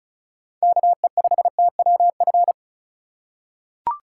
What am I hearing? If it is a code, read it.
Morse code, heard as a keyed tone, about 34 wpm, 700 Hz. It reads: KE5TWF
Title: USA Call Signs - 6 Characters - 3x Character Spacing - 35wpm